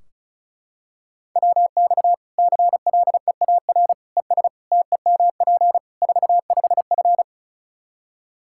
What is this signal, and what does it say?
Morse code, heard as a keyed tone, about 36 wpm, 700 Hz. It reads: WX CLEAR ES TEMP 45F
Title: Practice QSO - Exchange 2 - Info - 35wpm